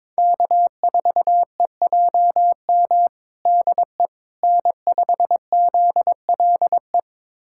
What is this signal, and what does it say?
Morse code, heard as a keyed tone, about 22 wpm, 700 Hz. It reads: K4EJM DE N5ZLE
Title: USA Callsign DE Callsign - 5 Characters - 22wpm